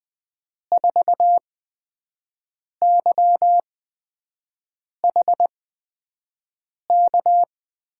Morse code, heard as a keyed tone, about 20 wpm, spaced as for 5 wpm, 700 Hz. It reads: 4YHK
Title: USA Call Signs - 6 Characters - 8x Character Spacing - 20wpm